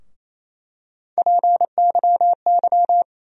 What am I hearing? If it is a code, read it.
Morse code, heard as a keyed tone, about 28 wpm, 700 Hz. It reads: PYY
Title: USA Call Sign Postfixes - 3 Letters 28wpm